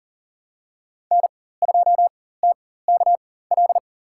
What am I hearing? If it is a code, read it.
Morse code, heard as a keyed tone, about 40 wpm, spaced as for 19 wpm, 700 Hz. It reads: N2TKL
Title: USA Call Signs - 4x Character Spacing - 40wpm